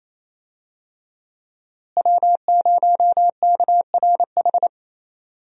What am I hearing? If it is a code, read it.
Morse code, heard as a keyed tone, about 28 wpm, 700 Hz. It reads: W0KRH
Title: USA Call Signs - 5 Characters - Repeated 2x - 28wpm